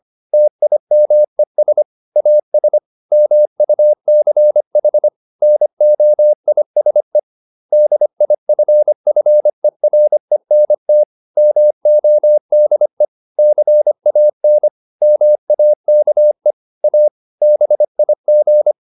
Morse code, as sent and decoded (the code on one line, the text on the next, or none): - .. -- . ... / .- ... / -- ..- -.-. .... / -. --- .. ... / -.. .. ..-. ..-. .-. -. - / -- --- -.. / -.-. .- -. / -- .- -.- . / .- / -... .. --.
TIMES AS MUCH NOISE DIFFERENT MODE CAN MAKE A BIG